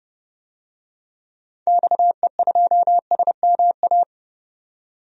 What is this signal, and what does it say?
Morse code, heard as a keyed tone, about 30 wpm, 700 Hz. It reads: XE2SMA
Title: DXCC-100 Call Signs - 6 Characters - 30wpm